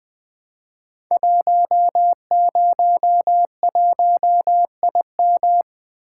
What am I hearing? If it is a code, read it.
Morse code, heard as a keyed tone, about 20 wpm, 700 Hz. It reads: 101IM
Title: Code Groups - 5 Letter:Numbers - 20wpm